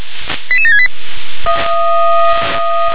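Five-tone selective call (ZVEI), heard as quick runs of five tones, tones of about 70 ms. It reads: 88768
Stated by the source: ZVEI five-tone selective call 88792 received on SDR.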